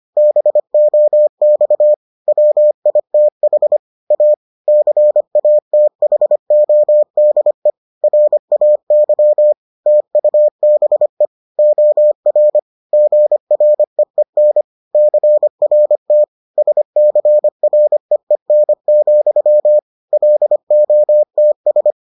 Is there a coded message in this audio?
Yes: Morse code from 0.2 to 21.9 s, heard as a keyed tone, about 25 wpm, 600 Hz. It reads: BOX WITH A CATHODE RAY TUBE OR GREEN CRT SCREEN, LOTS